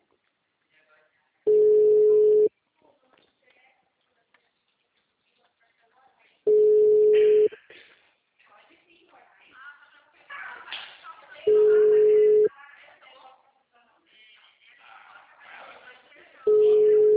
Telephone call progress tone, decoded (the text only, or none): ringback tone